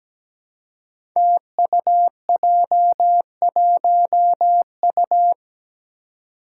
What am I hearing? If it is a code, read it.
Morse code, heard as a keyed tone, about 17 wpm, 700 Hz. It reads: TUJ1U
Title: Five Challenging Letter-Number Combinations 17wpm